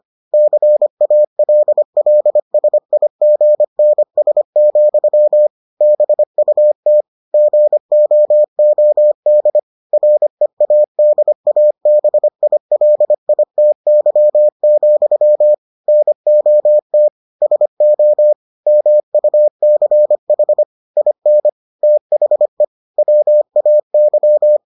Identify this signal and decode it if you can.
Morse code, heard as a keyed tone, about 25 wpm, 600 Hz. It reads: CALLSIGNS, BUT GOOD READABILITY, NOT SO MUCH IN THE WAY